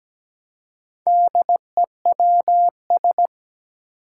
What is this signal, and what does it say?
Morse code, heard as a keyed tone, about 17 wpm, 700 Hz. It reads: DEWS
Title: Four Challenging Letter-Number Combinations 17wpm